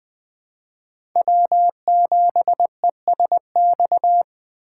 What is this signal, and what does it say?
Morse code, heard as a keyed tone, about 21 wpm, 700 Hz. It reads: W7ESX
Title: USA Call Signs - 5 Characters - 2x Character Spacing - 20wpm